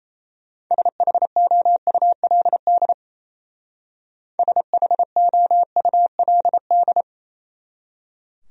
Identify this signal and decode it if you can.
Morse code, heard as a keyed tone, about 34 wpm, 700 Hz. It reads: SHOULD SHOULD
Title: Speed-Racing Top 500 Words 28wpm